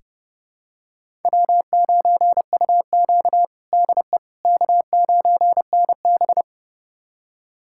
Morse code, heard as a keyed tone, about 30 wpm, 700 Hz. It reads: W9UQ DE K9NB
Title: USA Callsign DE Callsign - 4 Characters - 30wpm